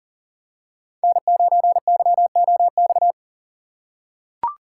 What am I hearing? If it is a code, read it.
Morse code, heard as a keyed tone, about 38 wpm, 700 Hz. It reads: N9YOX
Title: USA Call Signs - 5 Characters - 3x Character Spacing - 40wpm